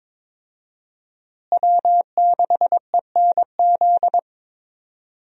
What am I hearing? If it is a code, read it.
Morse code, heard as a keyed tone, about 23 wpm, 700 Hz. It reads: W6ENZ
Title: USA Call Signs - 5 Characters - Repeated 3x - 22wpm